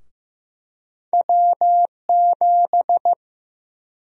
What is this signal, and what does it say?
Morse code, heard as a keyed tone, about 16 wpm, 700 Hz. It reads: W7
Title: USA Call Sign Prefixes - 15wpm